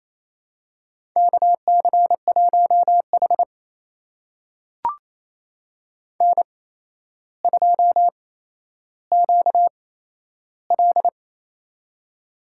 Morse code, heard as a keyed tone, about 28 wpm, spaced as for 12 wpm, 700 Hz. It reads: KC1H N2QL
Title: USA Call Signs - 4 Characters - 8x Character Spacing - 28wpm